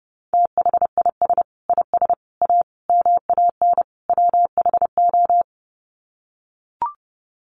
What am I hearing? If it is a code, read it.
Morse code, heard as a keyed tone, about 28 wpm, 700 Hz. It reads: THIS IS A MAN WHO
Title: Sets of 5 Words Encore2 30wpm